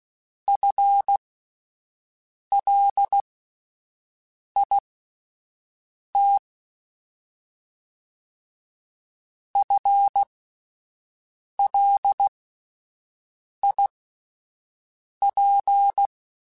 Morse code, heard as a keyed tone, about 16 wpm, spaced as for 5 wpm, 800 Hz. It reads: FLIT FLIP
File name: Alphabetical / Alphabetical mp3s / Fs @5